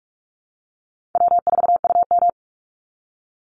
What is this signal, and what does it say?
Morse code, heard as a keyed tone, about 44 wpm, 700 Hz. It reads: W4UM
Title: Easy Call Signs - 45wpm